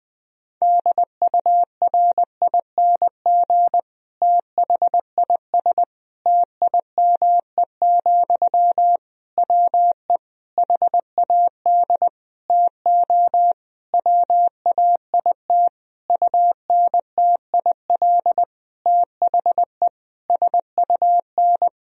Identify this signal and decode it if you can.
Morse code, heard as a keyed tone, about 20 wpm, 700 Hz. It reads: DURING THIS TIME, WE HAD TO WAIT UNTIL THE SUN
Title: Sentences from Top 1000 Words - Review Part 2 - 20wpm